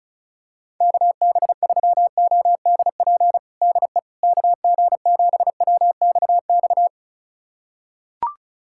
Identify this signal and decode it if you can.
Morse code, heard as a keyed tone, about 34 wpm, 700 Hz. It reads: KB3ODP DE KG7WXX
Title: USA Callsign DE Callsign 35wpm